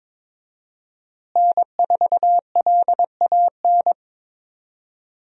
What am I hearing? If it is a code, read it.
Morse code, heard as a keyed tone, about 22 wpm, 700 Hz. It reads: N4LAN
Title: USA Call Signs - 5 Characters - Repeated 2x - 22wpm